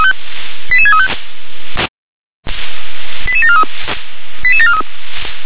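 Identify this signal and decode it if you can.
Five-tone selective call (ZVEI), heard as quick runs of five tones, tones of about 70 ms. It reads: 88635 88743 88743
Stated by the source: ZVEI five-tone call received on SDR.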